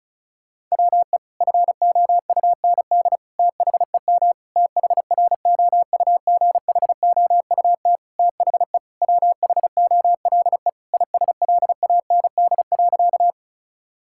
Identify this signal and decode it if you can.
Morse code, heard as a keyed tone, about 34 wpm, 700 Hz. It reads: WE FOUND THEM THROUGHOUT THE WHOLE ISLAND.
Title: Sentences from Top 1000 Words - Review Part 5 - 35wpm